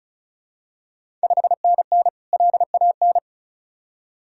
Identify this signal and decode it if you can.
Morse code, heard as a keyed tone, about 34 wpm, 700 Hz. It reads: 5NN LAN
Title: Ontario QSO Party Exchange 35wpm